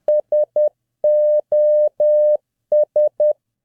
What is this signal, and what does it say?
Morse code, heard as a keyed tone, about 10 wpm, 600 Hz. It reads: SOS